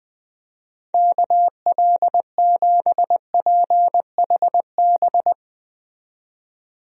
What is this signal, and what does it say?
Morse code, heard as a keyed tone, about 21 wpm, 700 Hz. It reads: KL7PHB
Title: DXCC-100 Call Signs - 6 Characters - 20wpm